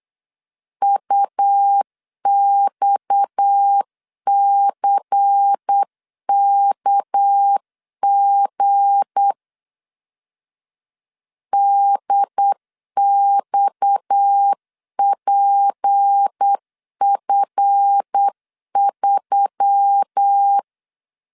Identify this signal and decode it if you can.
Morse code, heard as a keyed tone, about 8 wpm, 800 Hz. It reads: UXCKG DXPF3
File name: LEZIONI / 7 GF9Z / 7 GF 8wpm 3108